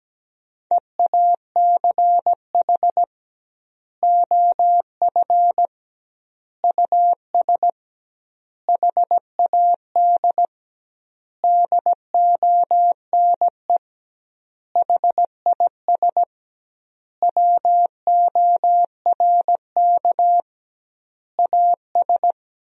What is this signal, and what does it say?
Morse code, heard as a keyed tone, about 17 wpm, 700 Hz. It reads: EACH OF US HAD DONE HIS WORK AS
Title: 2x Word-Spacing - Sentences from Top 500 Words - 17wpm